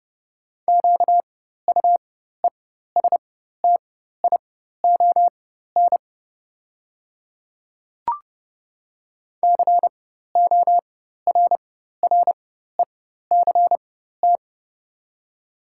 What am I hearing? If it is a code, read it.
Morse code, heard as a keyed tone, about 30 wpm, spaced as for 14 wpm, 700 Hz. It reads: QUESTION CORRECT
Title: 4x Character Spacing - Top 500 Words - 30wpm